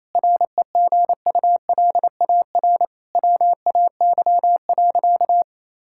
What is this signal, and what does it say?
Morse code, heard as a keyed tone, about 28 wpm, 700 Hz. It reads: REGULAR WAY.
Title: Sentences from Top 1000 Words - Review Part 6 - 28wpm